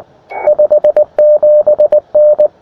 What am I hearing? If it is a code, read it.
Morse code, heard as a keyed tone, about 23 wpm, 530 Hz. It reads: 57N
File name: attachments / europe 20m